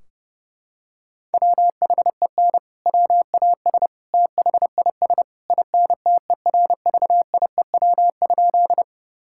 Morse code, heard as a keyed tone, about 29 wpm, 700 Hz. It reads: WHEN WAS THIS INTERVIEW?
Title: Sentences from Top 1600 Words 30wpm